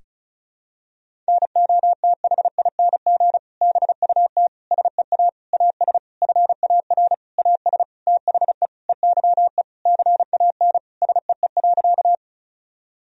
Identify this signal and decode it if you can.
Morse code, heard as a keyed tone, about 36 wpm, 700 Hz. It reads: NOTHING BUT SEA AS FAR AS THE EYE CAN SEE.
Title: Sentences from Top 1000 Words - Review Part 1 - 35wpm